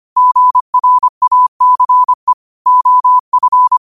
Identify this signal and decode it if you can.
Morse code, heard as a keyed tone, about 25 wpm, 1000 Hz. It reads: GRACE OF